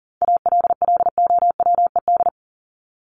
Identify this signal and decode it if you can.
Morse code, heard as a keyed tone, about 40 wpm, 700 Hz. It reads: ALLOWED
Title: ED Suffix - Top 100 Words with Base Words 40wpm